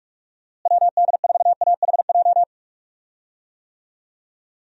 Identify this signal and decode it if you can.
Morse code, heard as a keyed tone, about 44 wpm, 700 Hz. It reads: WD4AHJ
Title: USA Call Signs - 6 Characters - Repeated 2x - 45wpm